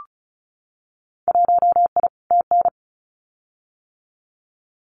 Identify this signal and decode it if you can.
Morse code, heard as a keyed tone, about 36 wpm, 700 Hz. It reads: 1I TN
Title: Winter Field Day Exchanges Encore 35wpm